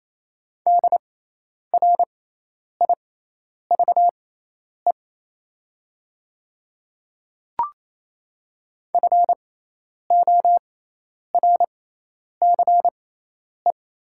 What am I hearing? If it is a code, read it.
Morse code, heard as a keyed tone, about 27 wpm, spaced as for 10 wpm, 700 Hz. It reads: DRIVE FORCE
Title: 6x Character Spacing - Top 500 Words - 28wpm